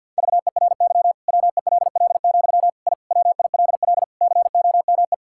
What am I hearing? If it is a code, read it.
Morse code, heard as a keyed tone, about 52 wpm, 700 Hz. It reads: VERY WELL, I WILL COME
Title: Sentences from Top 1000 Words - Review Part 5 - 50wpm